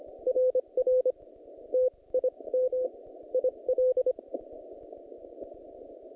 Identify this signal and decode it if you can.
Morse code, heard as a keyed tone, about 25 wpm, spaced as for 21 wpm, 510 Hz. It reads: RR TIM IL
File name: contests / naqp-cw-k5tr-op-k5pi- -0100